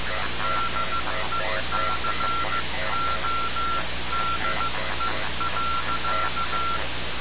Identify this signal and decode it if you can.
Morse code, heard as a keyed tone, about 15 wpm, 1360 Hz, with noise. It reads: DECODE.
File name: radiocap / REC z